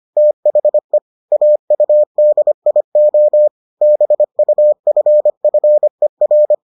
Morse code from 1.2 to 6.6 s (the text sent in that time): AUDIO BUFFER